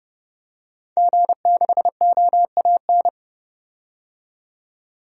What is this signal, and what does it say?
Morse code, heard as a keyed tone, about 30 wpm, 700 Hz. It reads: G6OAN
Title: DXCC-100 Call Signs - 5 Characters - 30wpm